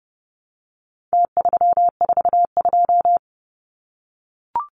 Morse code, heard as a keyed tone, about 30 wpm, 700 Hz. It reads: T342
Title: Numbers - Contest Style - 4-Digit 30wpm